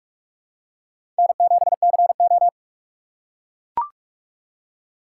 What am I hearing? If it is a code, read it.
Morse code, heard as a keyed tone, about 44 wpm, 700 Hz. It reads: N7CO
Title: USA Call Signs - 4 Characters - 2x Character Spacing - 45wpm